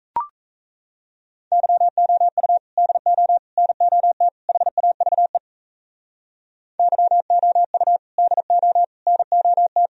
Morse code, heard as a keyed tone, about 36 wpm, 700 Hz. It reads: YOU DO NOT HAVE YOU DO NOT